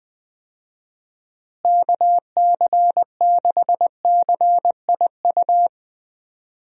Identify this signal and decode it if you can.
Morse code, heard as a keyed tone, about 20 wpm, 700 Hz. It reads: KC6CIU
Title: USA Call Signs - 6 Characters - Repeated 3x - 20wpm